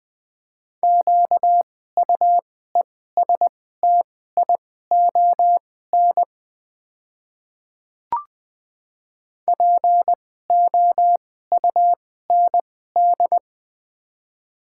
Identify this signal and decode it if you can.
Morse code, heard as a keyed tone, about 20 wpm, spaced as for 14 wpm, 700 Hz. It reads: QUESTION POUND
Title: 2x Character Spacing - Top 500 Words - 20wpm